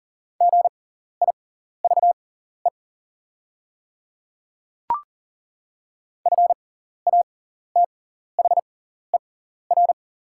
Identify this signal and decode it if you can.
Morse code, heard as a keyed tone, about 40 wpm, spaced as for 14 wpm, 700 Hz. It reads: GIVE FATHER
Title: 6x Character Spacing - Top 200 Words - 40wpm